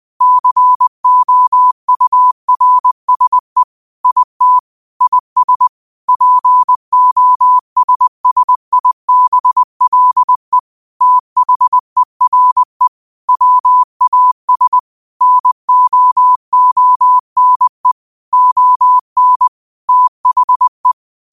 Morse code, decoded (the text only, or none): COURSE IT IS POSSIBLE THERE WAS NOONE ON THE